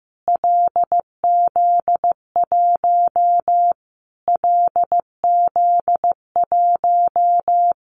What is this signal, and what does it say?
Morse code, heard as a keyed tone, about 15 wpm, 700 Hz. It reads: LZ1 LZ1
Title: Callsign Prefixes for Top 25 DXCC - Repeated 3x - 15wpm